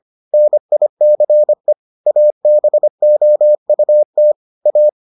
Morse code, heard as a keyed tone, about 25 wpm, 600 Hz. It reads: NICE ABOUT A